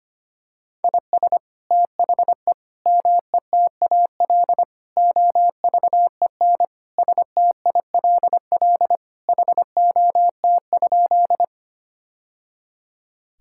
Morse code, heard as a keyed tone, about 25 wpm, 700 Hz. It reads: IS THE METAL OVEN STILL HOT?
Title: Sentences from Top 1000 Words - Review Part 6 - 25wpm